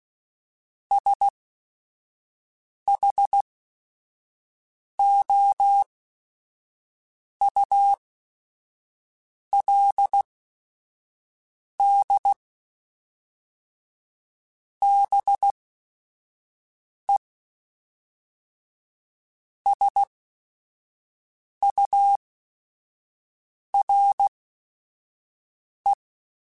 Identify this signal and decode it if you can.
Morse code, heard as a keyed tone, about 16 wpm, spaced as for 5 wpm, 800 Hz. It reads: SHOULD BE SURE
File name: Chuckles / Chuckles-mp3s / BrainMouth @5